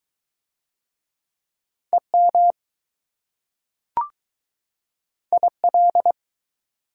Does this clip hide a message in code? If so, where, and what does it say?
Yes: Morse code from 1.9 to 6.1 s, heard as a keyed tone, about 22 wpm, 700 Hz. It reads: EM IL